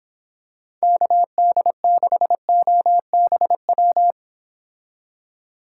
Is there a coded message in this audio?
Yes: Morse code from 0.8 to 4.1 s, heard as a keyed tone, about 26 wpm, 700 Hz. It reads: KD6OBW